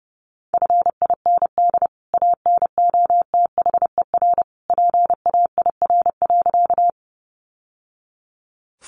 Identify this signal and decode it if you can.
Morse code, heard as a keyed tone, about 30 wpm, 700 Hz. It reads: FIND ANOTHER PAIR.
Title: Sentences from Top 1200 Words 30wpm